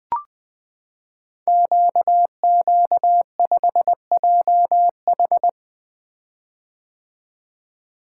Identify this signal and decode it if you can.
Morse code, heard as a keyed tone, about 21 wpm, 700 Hz. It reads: QQ5JH
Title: Code Groups - 5 Letter:Numbers - 20wpm